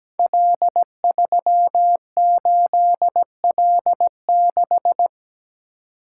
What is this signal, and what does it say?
Morse code, heard as a keyed tone, about 17 wpm, 700 Hz. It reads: L38L6
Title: Code Groups - 5 Letter:Numbers - 17wpm